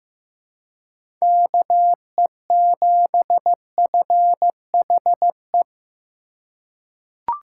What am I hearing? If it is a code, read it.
Morse code, heard as a keyed tone, about 15 wpm, 700 Hz. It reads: KE7FHE
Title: USA Call Signs - 6 Characters - 3x Character Spacing - 15wpm